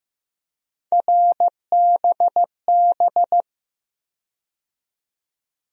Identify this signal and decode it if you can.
Morse code, heard as a keyed tone, about 15 wpm, 700 Hz. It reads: RBB
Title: USA Call Sign Postfixes - 3 Letters 15wpm